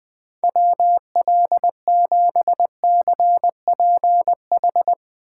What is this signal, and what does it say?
Morse code, heard as a keyed tone, about 23 wpm, 700 Hz. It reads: WL7CPH
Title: Speed Racing - USA 2x3 Call Signs 20wpm